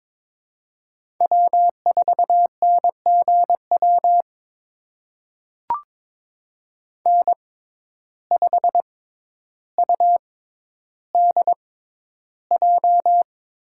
Morse code, heard as a keyed tone, about 22 wpm, spaced as for 11 wpm, 700 Hz. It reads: W4NGW N5UDJ
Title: USA Call Signs - 5 Characters - 6x Character Spacing - 22wpm